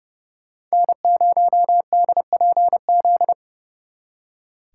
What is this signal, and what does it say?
Morse code, heard as a keyed tone, about 37 wpm, 700 Hz. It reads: N0DPZ
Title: Speed Racing - USA 1x3 Call Signs 30wpm